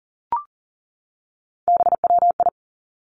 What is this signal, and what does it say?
Morse code, heard as a keyed tone, about 41 wpm, 700 Hz. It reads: BWI
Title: Three Challenging Letter-Number Combinations 40wpm